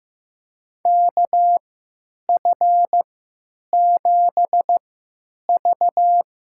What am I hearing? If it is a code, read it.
Morse code, heard as a keyed tone, about 15 wpm, spaced as for 9 wpm, 700 Hz. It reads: KF7V
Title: USA Call Signs - 3x Character Spacing - 15wpm